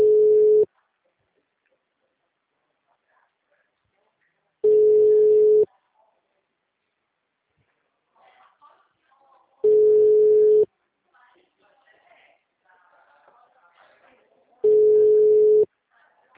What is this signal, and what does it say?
Ringback tone (Europe, 425 Hz), with speech.
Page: Atibaia - SP 1 (CP Cartao)